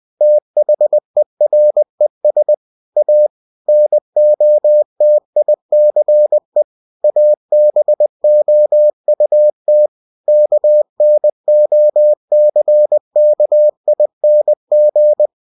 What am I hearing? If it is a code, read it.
Morse code, heard as a keyed tone, about 20 wpm, 600 Hz. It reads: THERES A NOTICE ABOUT KNOCKING
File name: cw-books / winnie-the-pooh / winnie-20wpm-page-0052